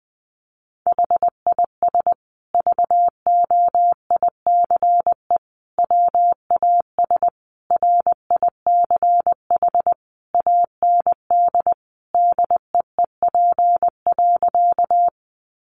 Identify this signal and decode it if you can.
Morse code, heard as a keyed tone, about 19 wpm, 700 Hz. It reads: HIS VOICE WAS RICH AND DEEP.
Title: Sentences from Top 1100 Words 20wpm